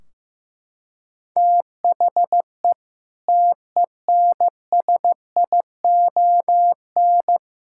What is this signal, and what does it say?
Morse code, heard as a keyed tone, about 15 wpm, 700 Hz. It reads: THE TENSION
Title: Sentences from Top 1600 Words 15wpm